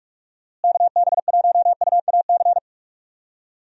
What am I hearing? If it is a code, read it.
Morse code, heard as a keyed tone, about 47 wpm, 700 Hz. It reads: KB1UAC
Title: USA Call Signs - 6 Characters - Repeated 3x - 45wpm